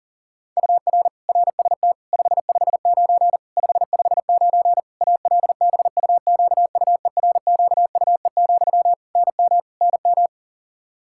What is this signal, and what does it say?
Morse code, heard as a keyed tone, about 40 wpm, 700 Hz. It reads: UR RST 559 559 ALBUQUERQUE, NM NM
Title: Practice QSO - Exchange 1 - Beginning - 40wpm